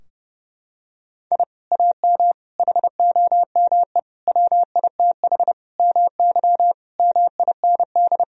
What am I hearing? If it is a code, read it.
Morse code, heard as a keyed tone, about 30 wpm, 700 Hz. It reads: I AM HOME WITH MY MIND
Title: Sentences from Top 1000 Words - Review Part 3 - 30wpm